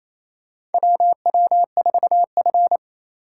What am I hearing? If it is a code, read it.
Morse code, heard as a keyed tone, about 28 wpm, 700 Hz. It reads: WW4F
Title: USA Call Signs - 4 Characters - 3x Character Spacing - 28wpm